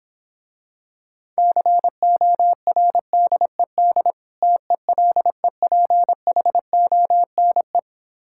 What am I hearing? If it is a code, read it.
Morse code, heard as a keyed tone, about 26 wpm, 700 Hz. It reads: CORDED TELEPHONE